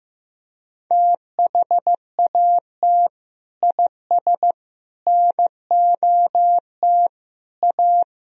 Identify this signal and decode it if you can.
Morse code, heard as a keyed tone, about 15 wpm, 700 Hz. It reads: THAT IS NOT A
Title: Sentences from Top 1000 Words - Review Part 3 - 15wpm